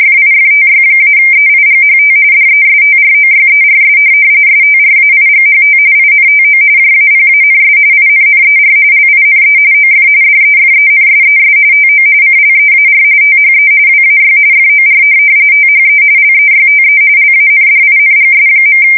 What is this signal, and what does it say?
RTTY, about 45.45 baud. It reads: FIFTY-EIGHT DESTROYERS STRAIGHT INTO JAPANS INNER LINE OF DEFENSES TO CLEAR THE WAY FOR THE LEYTE LANDINGS.
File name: fsk170 / USS-New-Jersey